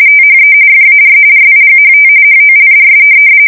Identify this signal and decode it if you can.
RTTY, about 45.45 baud. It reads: THE FAR RIGHT HOLDS
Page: 170 Shift 2125/2295